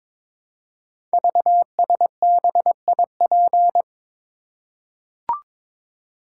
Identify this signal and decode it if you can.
Morse code, heard as a keyed tone, about 22 wpm, 690 Hz. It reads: VSBIP
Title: Code Groups - 5 Letter:Numbers - 22wpm